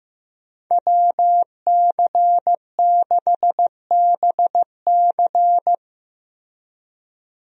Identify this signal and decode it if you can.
Morse code, heard as a keyed tone, about 15 wpm, 700 Hz. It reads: WC6BC
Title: USA Call Signs - 5 Characters - Repeated 3x - 15wpm